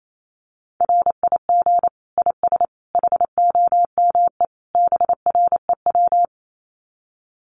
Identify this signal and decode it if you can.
Morse code, heard as a keyed tone, about 28 wpm, 700 Hz. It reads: RIG IS HOME BREW
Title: Common QSO Phrases 28wpm